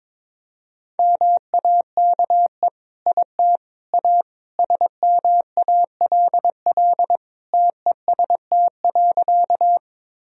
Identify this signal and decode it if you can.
Morse code, heard as a keyed tone, about 22 wpm, 700 Hz. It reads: MAKE IT A SMALL TEST.
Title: Sentences from Top 1000 Words - Review Part 3 - 22wpm